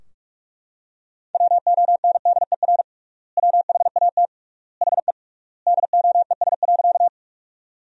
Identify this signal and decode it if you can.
Morse code, heard as a keyed tone, about 42 wpm, 700 Hz. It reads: WONDER WHAT HE DOES.
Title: 3x Word-Spacing - Sentences from Top 500 Words - 45wpm